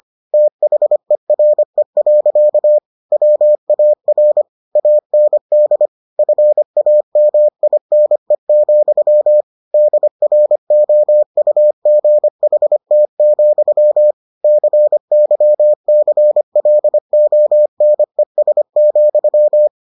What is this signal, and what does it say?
Morse code, heard as a keyed tone, about 25 wpm, 600 Hz. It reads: THERE. WAR AND FAMINE, DROUGHT, CYCLONES,